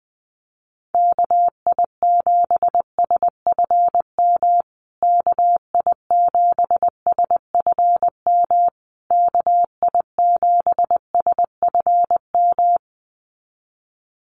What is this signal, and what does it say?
Morse code, heard as a keyed tone, about 20 wpm, 700 Hz. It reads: KI7SFM KI7SFM KI7SFM
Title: USA Call Signs Repeated 3x 20wpm